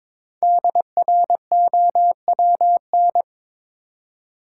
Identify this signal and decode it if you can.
Morse code, heard as a keyed tone, about 22 wpm, 700 Hz. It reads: DROWN
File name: 5 Letter Words - First 3 Letters Given - 22wpm